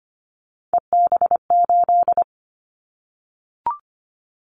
Morse code, heard as a keyed tone, about 25 wpm, 700 Hz. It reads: EB8
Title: Callsign Prefixes for Top 100 DXCC - 25wpm